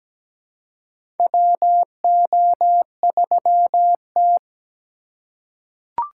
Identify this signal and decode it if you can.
Morse code, heard as a keyed tone, about 17 wpm, 700 Hz. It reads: WO3T
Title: USA Call Signs - 4 Characters - 3x Character Spacing - 17wpm